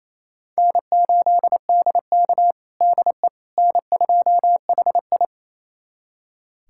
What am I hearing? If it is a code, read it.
Morse code, heard as a keyed tone, about 28 wpm, 700 Hz. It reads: N8DK DE N2HI
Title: USA Callsign DE Callsign - 4 Characters - 28wpm